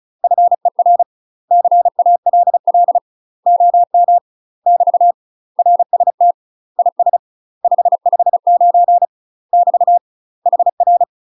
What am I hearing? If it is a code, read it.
Morse code, heard as a keyed tone, about 34 wpm, 700 Hz. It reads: FER CALL OM <BT> RST IS 559 <BT> HR
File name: UK MASTER QSOs TRAINING / 35WPM / UK MASTER QSOs 0024